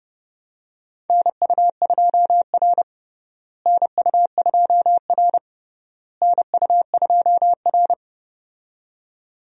Morse code, heard as a keyed tone, about 32 wpm, 700 Hz. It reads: NU2R NU2R NU2R
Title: USA Call Signs - 4 Characters - Repeated 3x - 30wpm